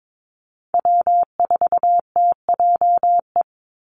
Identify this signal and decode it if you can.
Morse code, heard as a keyed tone, about 22 wpm, 700 Hz. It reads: W4TJE
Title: Call Signs - 22wpm